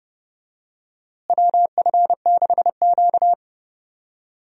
Morse code, heard as a keyed tone, about 31 wpm, 700 Hz. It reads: WF6Q
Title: USA Call Signs - 4 Characters - Repeated 3x - 30wpm